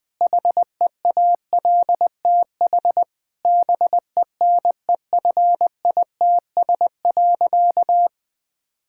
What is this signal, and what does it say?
Morse code, heard as a keyed tone, about 20 wpm, 700 Hz. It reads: HEALTH BENEFITS.
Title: Difficult Sentences with Hints - 20wpm